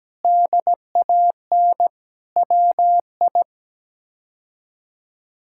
Morse code, heard as a keyed tone, about 17 wpm, 700 Hz. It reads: DAN WI
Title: K1USN SST QSO Practice - 17wpm